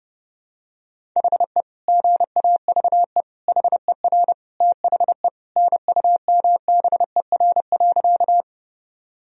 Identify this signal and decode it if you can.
Morse code, heard as a keyed tone, about 32 wpm, 700 Hz. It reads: HE GAVE HER THE NUMBER.
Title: Sentences from Top 1000 Words - Review Part 1 - 30wpm